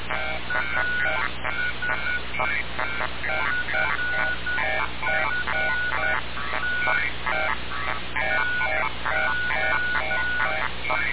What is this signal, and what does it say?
Morse code, heard as a keyed tone, about 10 wpm, 1460 Hz. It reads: AS FFT TO
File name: radiocap / REC z